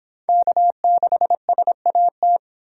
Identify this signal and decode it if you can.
Morse code, heard as a keyed tone, about 25 wpm, 700 Hz. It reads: K6SAT